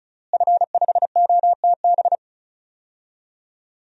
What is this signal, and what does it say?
Morse code, heard as a keyed tone, about 34 wpm, 700 Hz. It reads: F5OTB
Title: DXCC-100 Call Signs - 5 Characters - Repeated 3x - 35wpm